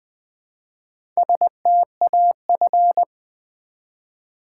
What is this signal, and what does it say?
Morse code, heard as a keyed tone, about 20 wpm, 700 Hz. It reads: STAF
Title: Build A Word - 1 Letter At A Time - 3 to 5-Letter Words - 20wpm